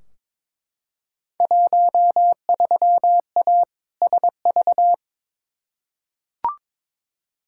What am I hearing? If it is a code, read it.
Morse code, heard as a keyed tone, about 22 wpm, 700 Hz. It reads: 13A SV
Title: Field Day Exchanges Encore - 22wpm